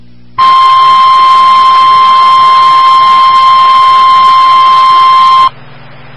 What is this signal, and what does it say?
Emergency Alert System (SAME) tones (the 1050 Hz attention tone), with noise. It is heard from 0.4 to 5.5 s.
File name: RR Skywarn / WX5OKC MHz Repeater